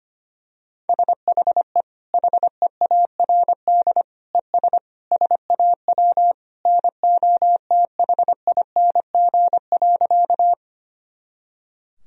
Morse code, heard as a keyed tone, about 25 wpm, 700 Hz. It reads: SHE HEARD ES SAW NOTHING.
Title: Sentences from Top 500 Words and Abbreviations 25wpm